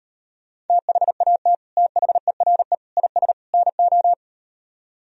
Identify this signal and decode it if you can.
Morse code, heard as a keyed tone, about 38 wpm, 700 Hz. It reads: THAT THERE IS NO